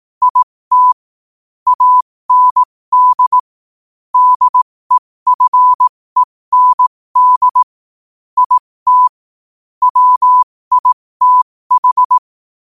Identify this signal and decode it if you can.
Morse code, heard as a keyed tone, about 18 wpm, spaced as for 15 wpm, 1000 Hz. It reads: IT AND DEFEND IT WITH
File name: qotdfiles / 15wpm